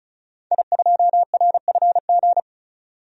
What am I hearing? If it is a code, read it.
Morse code, heard as a keyed tone, about 34 wpm, 700 Hz. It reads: I2RFG
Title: DXCC-100 Call Signs - 5 Characters - 35wpm